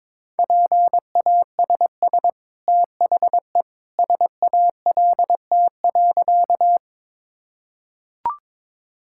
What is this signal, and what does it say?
Morse code, heard as a keyed tone, about 22 wpm, 690 Hz. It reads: PASS THE SALT.
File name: practice / 22wpm / Easy Sentences with Hints - 22wpm